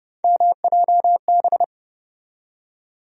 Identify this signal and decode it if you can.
Morse code, heard as a keyed tone, about 30 wpm, 700 Hz. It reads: MJB
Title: USA Call Sign Postfixes - 3 Letters 30wpm